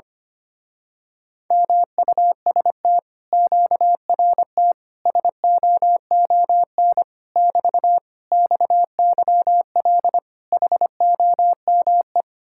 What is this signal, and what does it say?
Morse code, heard as a keyed tone, about 24 wpm, 700 Hz. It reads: MUST QRT SOON <BT> XYL HOME
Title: Speed-Racing Common QSO Phrases Encore - 25wpm